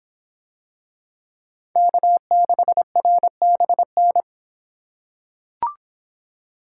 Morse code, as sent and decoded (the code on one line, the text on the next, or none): -.- -.... .-. -... -.
K6RBN